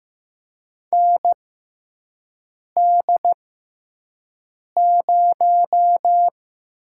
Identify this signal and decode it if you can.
Morse code, heard as a keyed tone, about 15 wpm, spaced as for 5 wpm, 700 Hz. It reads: ND0
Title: USA Call Signs - 4 Characters - 6x Character Spacing - 15wpm